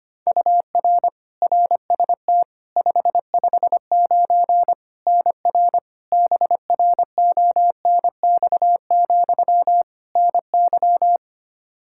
Morse code, heard as a keyed tone, about 25 wpm, 700 Hz. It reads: UR RST 559 NR BRONX, NY
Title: K1USN SST QSO Practice 25wpm